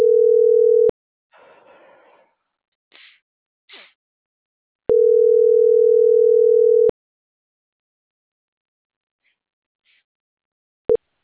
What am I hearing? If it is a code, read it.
Ringback tone (North America), 440 + 480 Hz.